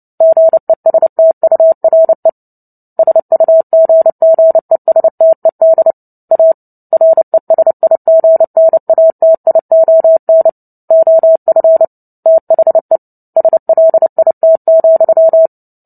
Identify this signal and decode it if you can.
Morse code, heard as a keyed tone, about 30 wpm, 650 Hz. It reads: GESTURE SUGGESTED A RESIGNATION OF THE SLIT,
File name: Book Two / 03-Chapter 3 - 29 WPM